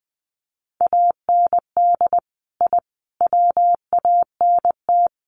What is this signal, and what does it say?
Morse code, heard as a keyed tone, about 20 wpm, 700 Hz. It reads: AND I WANT
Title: Sets of 5 Words Encore 20wpm